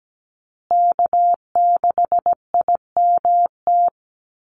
Morse code, heard as a keyed tone, about 17 wpm, 700 Hz. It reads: K6IMT
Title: USA Call Signs Repeated 3x 17wpm